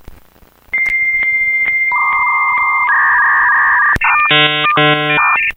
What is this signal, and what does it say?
Dial-up modem handshake.